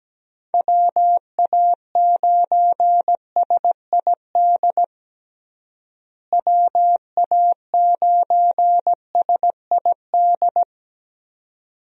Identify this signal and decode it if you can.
Morse code, heard as a keyed tone, about 17 wpm, 700 Hz. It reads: WA9SID WA9SID
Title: USA Call Signs - 6 Characters - Repeated 2x - 17wpm